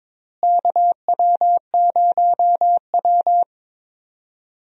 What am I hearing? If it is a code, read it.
Morse code, heard as a keyed tone, about 21 wpm, 700 Hz. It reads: KW0W
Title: USA Call Signs - 4 Characters - Repeated 3x - 22wpm